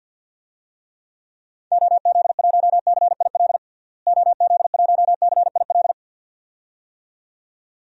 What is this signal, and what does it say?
Morse code, heard as a keyed tone, about 52 wpm, 700 Hz. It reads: OZ1CIL OZ1CIL
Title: DXCC-100 Call Signs - 6 Characters - Repeated 2x - 50wpm